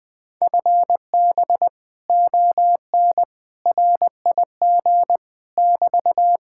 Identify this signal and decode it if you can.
Morse code, heard as a keyed tone, about 20 wpm, 700 Hz. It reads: FB ON RIG <BT>
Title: Speed-Racing Common QSO Phrases Encore - 17wpm